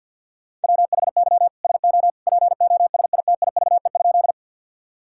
Morse code, heard as a keyed tone, about 51 wpm, 690 Hz. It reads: WHY SO POSITIVE?
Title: Sentences from Top 1000 Words - Review Part 5 - 50wpm